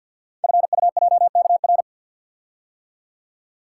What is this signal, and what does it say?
Morse code, heard as a keyed tone, about 52 wpm, 700 Hz. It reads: FUJKR
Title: Code Groups - 5 Letters - 50wpm